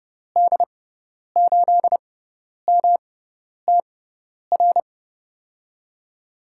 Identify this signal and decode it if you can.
Morse code, heard as a keyed tone, about 30 wpm, spaced as for 10 wpm, 700 Hz. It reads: D8MTR
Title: USA Call Signs - 6 Characters - 6x Character Spacing - 30wpm